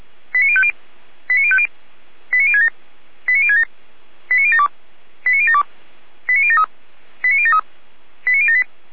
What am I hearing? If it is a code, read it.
Five-tone selective call (ZVEI), heard as quick runs of five tones, tones of about 70 ms. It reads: 89050 89050 89067 89067 89072 89072 89073 89073 89078